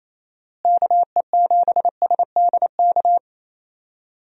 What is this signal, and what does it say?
Morse code, heard as a keyed tone, about 28 wpm, 700 Hz. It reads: KE7SDK
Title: USA Call Signs - 6 Characters - Repeated 2x - 28wpm